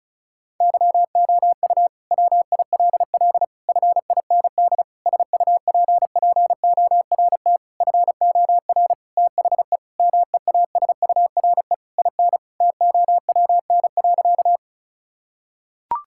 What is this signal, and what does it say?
Morse code, heard as a keyed tone, about 36 wpm, 700 Hz. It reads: YOU WILL FIND SUPPORT FOR THE MEASURE IN TOWN.
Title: Sentences from Top 1000 Words - Review Part 5 - 35wpm